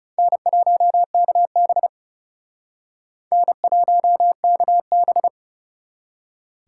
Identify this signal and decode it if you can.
Morse code, heard as a keyed tone, about 36 wpm, 700 Hz. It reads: N1KB N1KB
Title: Speed Racing - USA 1x2 Call Signs 30wpm